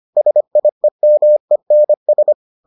Morse code, heard as a keyed tone, about 25 wpm, 600 Hz. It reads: SIEMENS